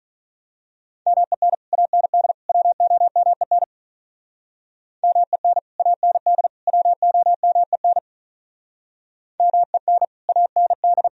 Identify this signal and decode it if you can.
Morse code, heard as a keyed tone, about 42 wpm, 700 Hz. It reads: MEN AND WOMEN MEN AND WOMEN MEN AND
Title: Speed-Racing - Sets of 3 Words 35wpm